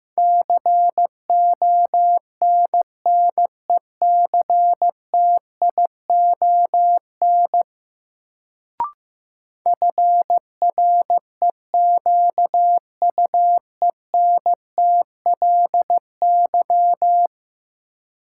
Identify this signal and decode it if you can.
Morse code, heard as a keyed tone, about 15 wpm, 700 Hz. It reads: CONNECTION FREQUENTLY
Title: Sentences from Top 1300 Words - New Word Drill Alt - 15wpm